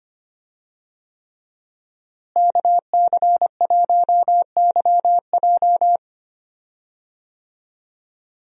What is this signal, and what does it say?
Morse code, heard as a keyed tone, about 25 wpm, 700 Hz. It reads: KC1YJ
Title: USA Call Signs - 5 Characters - Repeated 3x - 25wpm